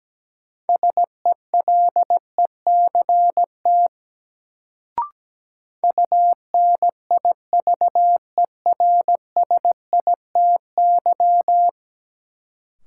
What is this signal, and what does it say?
Morse code, heard as a keyed tone, about 17 wpm, 700 Hz. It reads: SELECT UNIVERSITY
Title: Sentences from Top 1300 Words - New Word Drill Alt - 17wpm